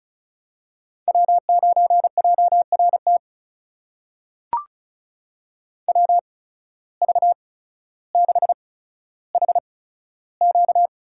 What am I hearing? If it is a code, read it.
Morse code, heard as a keyed tone, about 36 wpm, spaced as for 15 wpm, 700 Hz. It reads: W9JRT WV6HQ
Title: USA Call Signs - 5 Characters - 8x Character Spacing - 35wpm